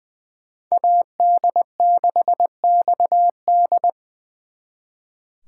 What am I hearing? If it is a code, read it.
Morse code, heard as a keyed tone, about 24 wpm, 700 Hz. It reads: AD6XD
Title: Speed Racing - USA 2x2 Call Signs 20wpm